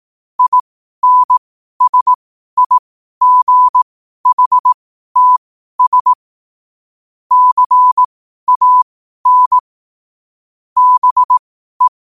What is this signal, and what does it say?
Morse code, heard as a keyed tone, about 18 wpm, spaced as for 12 wpm, 1000 Hz. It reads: INSIGHTS CAN BE
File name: qotdfiles / 12wpm